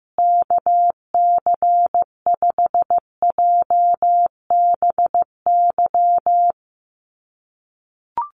Morse code, heard as a keyed tone, about 15 wpm, 700 Hz. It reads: KC5JBY
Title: USA Call Signs Repeated 3x 15wpm